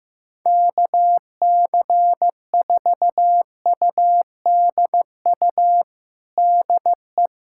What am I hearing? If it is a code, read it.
Morse code, heard as a keyed tone, about 14 wpm, 700 Hz. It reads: KC4UDU DE
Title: USA Callsign DE Callsign - 6 Characters - 15wpm